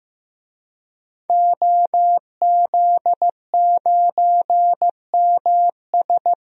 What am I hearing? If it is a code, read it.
Morse code, heard as a keyed tone, about 15 wpm, 700 Hz. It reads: OZ9MS
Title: DXCC-100 Call Signs - 5 Characters - 15wpm